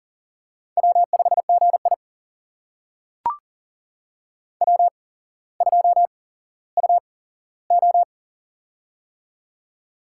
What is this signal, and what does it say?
Morse code, heard as a keyed tone, about 40 wpm, spaced as for 17 wpm, 700 Hz. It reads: W5GI W2UO
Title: USA Call Signs - 4 Characters - 8x Character Spacing - 40wpm